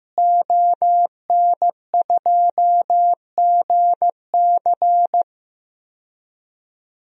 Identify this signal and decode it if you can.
Morse code, heard as a keyed tone, about 15 wpm, 700 Hz. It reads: ON2GC
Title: DXCC-100 Call Signs - 5 Characters - Repeated 3x - 15wpm